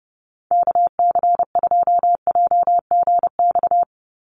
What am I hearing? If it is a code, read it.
Morse code, heard as a keyed tone, about 30 wpm, 700 Hz. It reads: KC2JGX
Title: USA Call Signs 30wpm